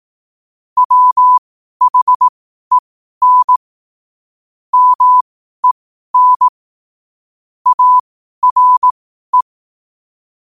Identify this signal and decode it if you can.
Morse code, heard as a keyed tone, about 18 wpm, spaced as for 12 wpm, 1000 Hz. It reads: WHEN MEN ARE
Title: 12 WPM morse code quotes for Thu, 29 Feb 2024